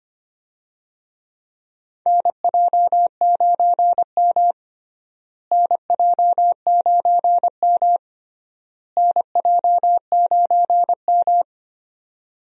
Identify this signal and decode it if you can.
Morse code, heard as a keyed tone, about 25 wpm, 700 Hz. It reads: NJ9M NJ9M NJ9M
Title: USA Call Signs - 4 Characters - Repeated 3x - 25wpm